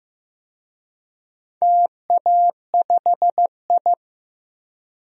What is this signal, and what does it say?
Morse code, heard as a keyed tone, about 15 wpm, 700 Hz. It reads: TA5I
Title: DXCC-100 Call Signs - 4 Characters - Repeated 2x - 15wpm